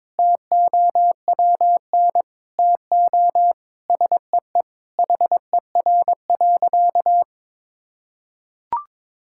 Morse code, heard as a keyed tone, about 22 wpm, 700 Hz. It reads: TOWN TO SEE HER.
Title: Sentences from Top 1000 Words - Review Part 5 - 22wpm